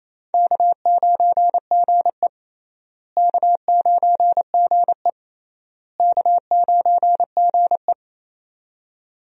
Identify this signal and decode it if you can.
Morse code, heard as a keyed tone, about 28 wpm, 700 Hz. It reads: K9GE K9GE K9GE
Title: USA Call Signs - 4 Characters - Repeated 3x - 28wpm